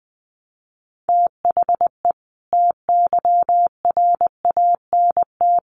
Morse code, heard as a keyed tone, about 20 wpm, 700 Hz. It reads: THE TYRANT
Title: A Sound Of Thunder 20wpm